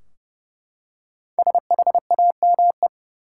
Morse code, heard as a keyed tone, about 31 wpm, 700 Hz. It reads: SHAME
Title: Five Letter Words - 30wpm